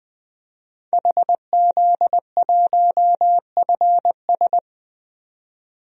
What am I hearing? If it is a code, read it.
Morse code, heard as a keyed tone, about 20 wpm, 700 Hz. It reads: HZ1FS
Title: DXCC-100 Call Signs - 5 Characters - 20wpm